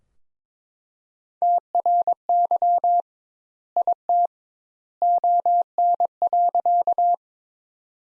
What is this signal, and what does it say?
Morse code, heard as a keyed tone, about 22 wpm, 700 Hz. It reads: TRY IT ON.
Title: Sentences Top 200 Words - 22wpm